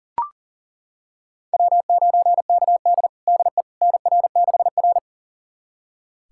Morse code, heard as a keyed tone, about 40 wpm, 700 Hz. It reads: W9KD DE NR6R
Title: USA Callsign DE Callsign - 4 Characters - 40wpm